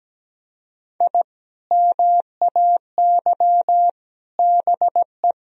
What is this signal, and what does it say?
Morse code, heard as a keyed tone, about 17 wpm, 700 Hz. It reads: I MAY BE
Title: Sentences from Top 1000 Words - Review Part 4 - 17wpm